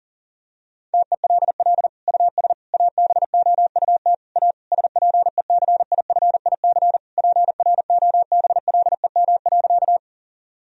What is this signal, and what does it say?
Morse code, heard as a keyed tone, about 40 wpm, 700 Hz. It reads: TELL US ABOUT A SPECIFIC PROBLEM.
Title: Sentences from Top 1000 Words - Review Part 3 - 40wpm